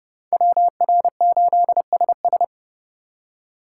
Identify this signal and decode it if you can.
Morse code, heard as a keyed tone, about 28 wpm, 700 Hz. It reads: WR8SS